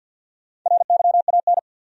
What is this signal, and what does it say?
Morse code, heard as a keyed tone, about 52 wpm, 700 Hz. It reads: RYAN
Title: USA Common Names - Max 4 Letters - 50wpm